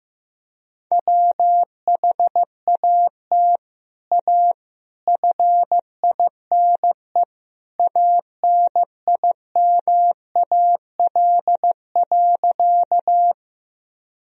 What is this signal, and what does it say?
Morse code, heard as a keyed tone, about 15 wpm, 700 Hz. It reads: WHAT A FINE ANIMAL.
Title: Sentences from Top 1000 Words - Review Part 1 - 15wpm